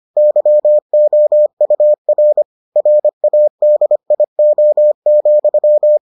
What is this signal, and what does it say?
Morse code, heard as a keyed tone, about 25 wpm, 600 Hz. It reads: YOUR RADIO,